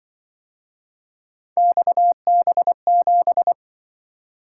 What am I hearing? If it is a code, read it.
Morse code, heard as a keyed tone, about 24 wpm, 700 Hz. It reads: XB7
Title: Speed-Racing - Three Letter-Number Combinations - 20wpm